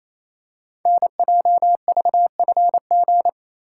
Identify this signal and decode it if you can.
Morse code, heard as a keyed tone, about 29 wpm, 700 Hz. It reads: NJVFG
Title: Code Groups - 5 Letters - 28wpm